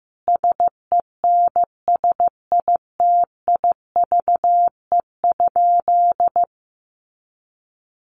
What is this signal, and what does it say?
Morse code, heard as a keyed tone, about 15 wpm, 700 Hz. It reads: SENSITIVE?
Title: Sentences from Top 1400 Words 15wpm